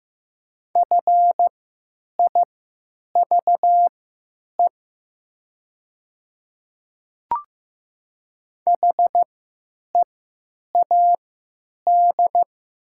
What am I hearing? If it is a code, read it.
Morse code, heard as a keyed tone, about 15 wpm, spaced as for 9 wpm, 700 Hz. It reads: FIVE HEAD
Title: 3x Character Spacing - Top 500 Words - 15wpm